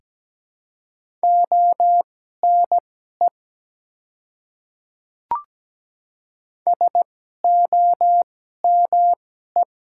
Morse code, heard as a keyed tone, about 17 wpm, spaced as for 12 wpm, 700 Hz. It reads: ONE SOME
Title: 2x Character Spacing - Top 100 Words - 17wpm